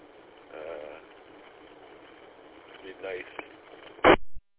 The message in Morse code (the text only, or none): none